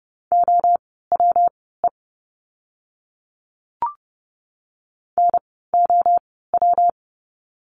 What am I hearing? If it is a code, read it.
Morse code, heard as a keyed tone, about 29 wpm, spaced as for 17 wpm, 700 Hz. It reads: OWE NOW
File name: course / 30:10wpm / 115 Lesson 19f - Letter w - Words 30wpm